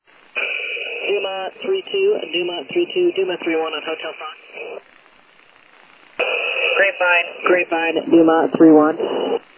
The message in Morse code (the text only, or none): none